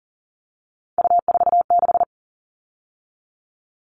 Morse code, heard as a keyed tone, about 38 wpm, 700 Hz. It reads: U46
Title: All Cut Numbers - 3-Digit 40wpm